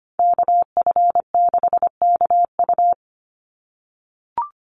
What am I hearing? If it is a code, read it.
Morse code, heard as a keyed tone, about 25 wpm, 690 Hz. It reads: KF6KU
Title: USA Call Signs 25wpm